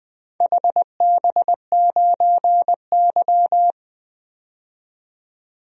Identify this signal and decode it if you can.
Morse code, heard as a keyed tone, about 20 wpm, 700 Hz. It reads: HB9Y
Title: DXCC-100 Call Signs - 4 Characters - 20wpm